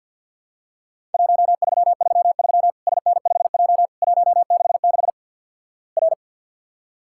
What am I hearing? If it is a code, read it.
Morse code, heard as a keyed tone, about 50 wpm, 700 Hz. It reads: 1333 SN5J 166 R
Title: Worked All Europe DX Contest - QTC Exchange with Repeats - 50wpm